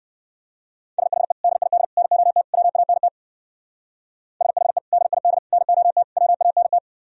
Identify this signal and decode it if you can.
Morse code, heard as a keyed tone, about 58 wpm, 700 Hz. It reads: SHE DID NOT WANT SHE DID NOT WANT